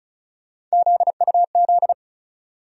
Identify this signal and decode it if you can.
Morse code, heard as a keyed tone, about 36 wpm, 700 Hz. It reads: ZUZ
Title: USA Call Sign Postfixes - 3 Letters 35wpm